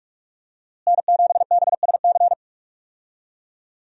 Morse code, heard as a keyed tone, about 46 wpm, 700 Hz. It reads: N7BSC
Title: USA Call Signs - 5 Characters - Repeated 2x - 45wpm